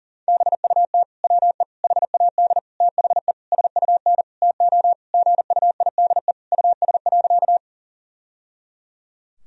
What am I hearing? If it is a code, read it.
Morse code, heard as a keyed tone, about 40 wpm, 700 Hz. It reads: BUT WE HAD THE SUN TO GUIDE US.
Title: Sentences from Top 1000 Words - Review Part 3 - 40wpm